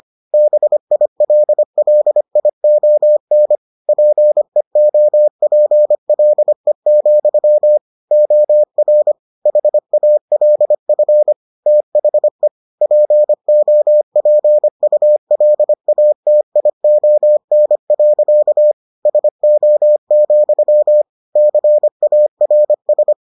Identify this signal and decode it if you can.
Morse code, heard as a keyed tone, about 25 wpm, 600 Hz. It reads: BILLION PEOPLE, OR HALF THE POPULATION. SO, CARS